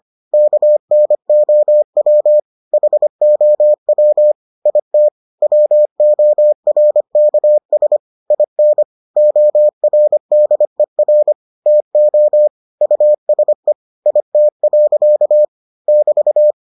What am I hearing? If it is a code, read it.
Morse code, heard as a keyed tone, about 25 wpm, 600 Hz. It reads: KNOW HOW IT WORKS IN ORDER TO USE IT. <BT>